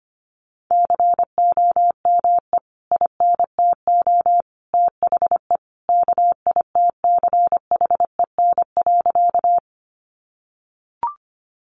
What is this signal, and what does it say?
Morse code, heard as a keyed tone, about 25 wpm, 700 Hz. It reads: COME INTO THE KITCHEN.
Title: Sentences from Top 1400 Words 25wpm